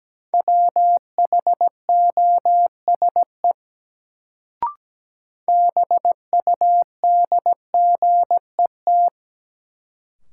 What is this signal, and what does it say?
Morse code, heard as a keyed tone, about 17 wpm, 700 Hz. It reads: WHOSE BUDGET
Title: Sentences from Top 1200 Words - New Word Drill Alt - 17wpm